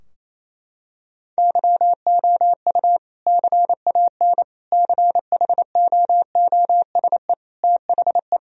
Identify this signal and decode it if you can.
Morse code, heard as a keyed tone, about 28 wpm, 700 Hz. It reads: YOU CAN CHOOSE THE
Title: Sentences from Top 1000 Words - Review Part 4 - 28wpm